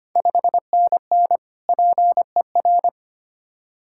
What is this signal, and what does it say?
Morse code, heard as a keyed tone, about 25 wpm, 700 Hz. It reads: 5NN PER
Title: Ontario QSO Party Exchange 25wpm